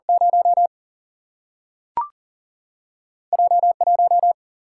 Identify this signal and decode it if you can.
Morse code, heard as a keyed tone, about 40 wpm, 700 Hz. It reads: 0 J1
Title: Single-Letter Single-Number 40wpm